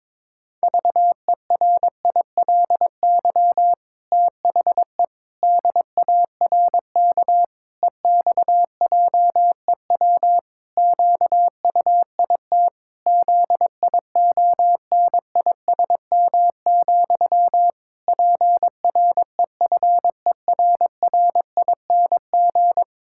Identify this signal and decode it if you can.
Morse code, heard as a keyed tone, about 22 wpm, 700 Hz. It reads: VERILY THE DARK EXJEW QUIT ZIONISM, PREFERRING
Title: Pangram Challenge v2 - 22wpm